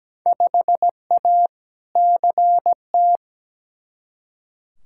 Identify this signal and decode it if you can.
Morse code, heard as a keyed tone, about 17 wpm, 700 Hz. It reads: 5A CT
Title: Field Day Exchanges - 17wpm